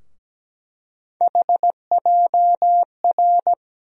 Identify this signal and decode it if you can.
Morse code, heard as a keyed tone, about 17 wpm, 700 Hz. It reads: HJR
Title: USA Call Sign Postfixes - 3 Letters 17wpm